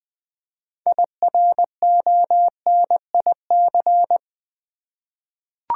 Morse code, heard as a keyed tone, about 20 wpm, 700 Hz. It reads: IRONIC
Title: 6 Letter Words - First 3 Letters Given - 20wpm